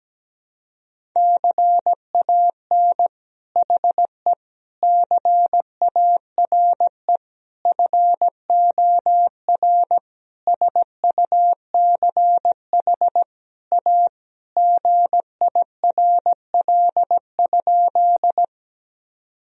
Morse code, heard as a keyed tone, about 17 wpm, 700 Hz. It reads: CAN HE CARE FOR SUCH A GIRL?
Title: Sentences from Top 1000 Words - Review Part 1 - 17wpm